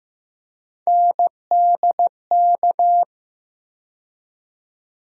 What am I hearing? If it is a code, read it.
Morse code, heard as a keyed tone, about 15 wpm, 700 Hz. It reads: NDK
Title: USA Call Sign Postfixes - 3 Letters 15wpm